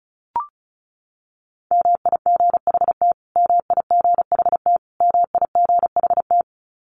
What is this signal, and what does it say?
Morse code, heard as a keyed tone, about 34 wpm, 700 Hz. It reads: MIGHT MIGHT MIGHT
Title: Top 500 Words Repeated 35wpm